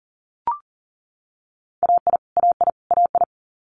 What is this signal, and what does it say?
Morse code, heard as a keyed tone, about 40 wpm, 700 Hz. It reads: AI AI AI
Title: Multi-letter Phonemes Repeated 3x 40wpm